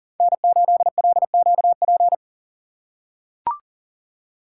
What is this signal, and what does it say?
Morse code, heard as a keyed tone, about 42 wpm, 700 Hz. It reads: N8LQP
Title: USA Call Signs - 5 Characters - 6x Character Spacing - 40wpm